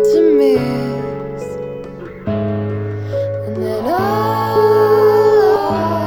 Ringback tone (Europe, 425 Hz), with noise.